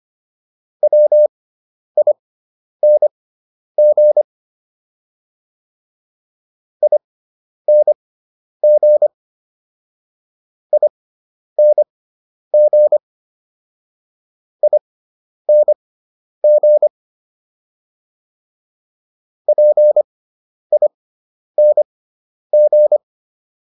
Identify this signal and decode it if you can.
Morse code, heard as a keyed tone, about 25 wpm, spaced as for 10 wpm, 600 Hz. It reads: WING ING ING ING PING